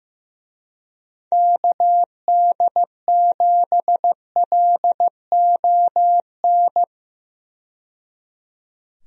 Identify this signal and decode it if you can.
Morse code, heard as a keyed tone, about 16 wpm, 700 Hz. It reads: KD7LON
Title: USA Call Signs - 6 Characters - 15wpm